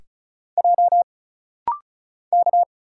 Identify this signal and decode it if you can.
Morse code, heard as a keyed tone, about 36 wpm, 700 Hz. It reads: J K
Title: Single Letters - Rapid-Fire 35wpm